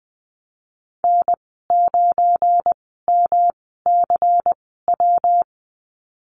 Morse code, heard as a keyed tone, about 20 wpm, spaced as for 14 wpm, 700 Hz. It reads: N9MCW
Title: 173 Lesson 28g - Number 9 - Call Signs 20wpm